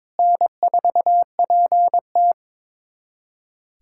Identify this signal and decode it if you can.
Morse code, heard as a keyed tone, about 22 wpm, 700 Hz. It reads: N4PT
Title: USA Callsign DE Callsign - 4 Characters - 22wpm